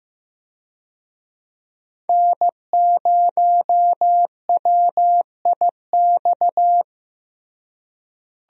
Morse code, heard as a keyed tone, about 15 wpm, 700 Hz. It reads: N0WIX
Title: USA Call Signs - 5 Characters - Repeated 3x - 15wpm